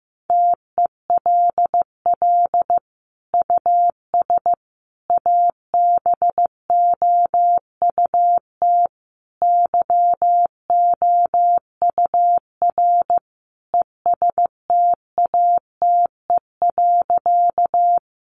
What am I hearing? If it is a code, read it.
Morse code, heard as a keyed tone, about 14 wpm, 700 Hz. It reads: TELL US ABOUT YOUR ESTATE.
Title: Sentences from Top 1200 Words 15wpm